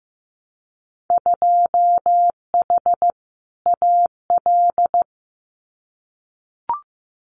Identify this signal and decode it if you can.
Morse code, heard as a keyed tone, about 15 wpm, 700 Hz. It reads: 2H AL
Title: Winter Field Day Exchanges 15wpm